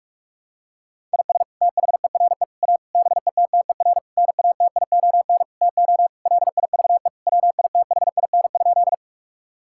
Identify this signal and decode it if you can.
Morse code, heard as a keyed tone, about 41 wpm, 700 Hz. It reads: IS THERE A BETTER NATION TO LIVE WITHIN?
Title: Sentences from Top 1000 Words - Review Part 6 - 45wpm